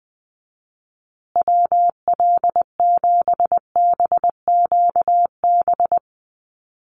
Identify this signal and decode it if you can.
Morse code, heard as a keyed tone, about 20 wpm, 700 Hz. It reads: WL7BQB
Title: USA Call Signs Repeated 3x 20wpm